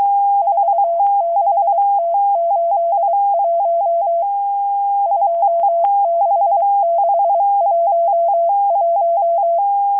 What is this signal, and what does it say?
Morse code, heard as a keyed tone, about 23 wpm, 680 Hz. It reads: 46T81 26611
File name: HM02- -0437utc-7351khz